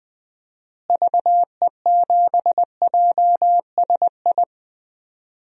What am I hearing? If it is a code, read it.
Morse code, heard as a keyed tone, about 19 wpm, 700 Hz. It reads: VE7JSI
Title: DXCC-100 Call Signs - 6 Characters - Repeated 3x - 20wpm